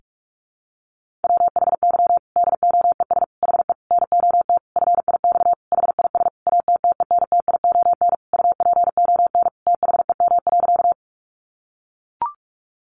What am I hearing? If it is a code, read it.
Morse code, heard as a keyed tone, about 44 wpm, 700 Hz. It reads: WHY DOES HE NOT FIX HIS ATTENTION UPON THEM.
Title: Sentences from Top 1400 Words 45wpm